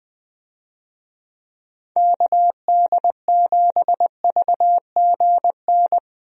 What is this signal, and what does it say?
Morse code, heard as a keyed tone, about 21 wpm, 700 Hz. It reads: KD7VGN
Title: USA Call Signs - 6 Characters - Repeated 2x - 20wpm